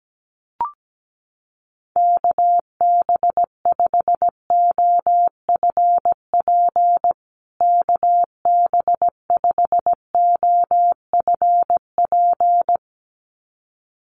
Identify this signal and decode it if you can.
Morse code, heard as a keyed tone, about 17 wpm, 700 Hz. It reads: KB5OFP KB5OFP
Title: USA Call Signs Repeated 2x 17wpm